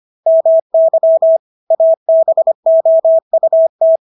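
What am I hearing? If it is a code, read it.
Morse code, heard as a keyed tone, about 25 wpm, 650 Hz. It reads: MY ABOUT
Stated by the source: Morse practice of English words at 25 wpm from CW med Gnister.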